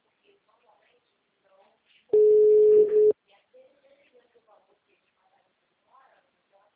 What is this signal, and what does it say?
Ringback tone (Europe, 425 Hz), with speech.
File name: out- -2024- - - .22990